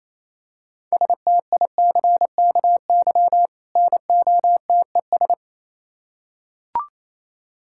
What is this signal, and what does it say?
Morse code, heard as a keyed tone, about 27 wpm, 700 Hz. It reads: STICKY NOTES
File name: Easy Words - Speed-Racing with Hints - 22wpm